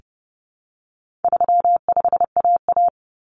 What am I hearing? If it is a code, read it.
Morse code, heard as a keyed tone, about 30 wpm, 700 Hz. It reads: 35AA
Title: Common Cut Numbers - 4-Digit 30wpm